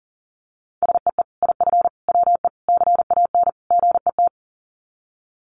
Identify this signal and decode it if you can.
Morse code, heard as a keyed tone, about 40 wpm, 700 Hz. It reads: SEE IF WE CAN GET
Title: Sets of 5 Words Encore2 40wpm